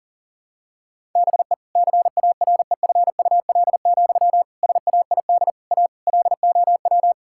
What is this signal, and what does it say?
Morse code, heard as a keyed tone, about 37 wpm, 700 Hz. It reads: BE CAREFUL, SAID A LOW
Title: Sentences from Top 1600 Words 40wpm